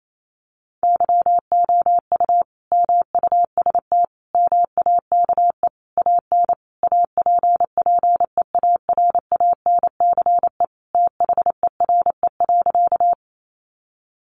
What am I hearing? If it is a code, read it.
Morse code, heard as a keyed tone, about 28 wpm, 700 Hz. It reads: YOU MUST MAKE AN APPEARANCE THERE.
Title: Sentences from Top 1100 Words 28wpm